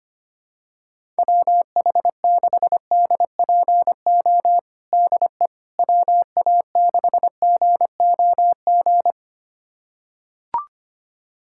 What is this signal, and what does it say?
Morse code, heard as a keyed tone, about 25 wpm, 700 Hz. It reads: WH6DPO DE WA6GOG
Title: USA Callsign DE Callsign 25wpm